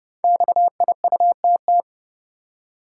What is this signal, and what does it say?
Morse code, heard as a keyed tone, about 29 wpm, 700 Hz. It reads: XIUTT
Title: Code Groups - 5 Letters - 30wpm